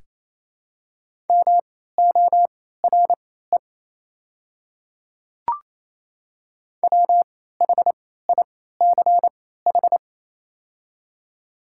Morse code, heard as a keyed tone, about 28 wpm, spaced as for 16 wpm, 700 Hz. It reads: MORE WHICH
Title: 3x Character Spacing - Top 200 Words - 28wpm